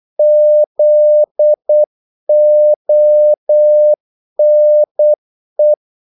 Morse code, heard as a keyed tone, about 8 wpm, 600 Hz. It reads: ZONE
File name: Korte ord DK 8wpm